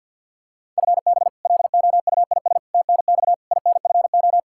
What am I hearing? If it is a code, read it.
Morse code, heard as a keyed tone, about 51 wpm, 700 Hz. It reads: FB LOUIS TNX INFO
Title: K1USN SST QSO Practice 50wpm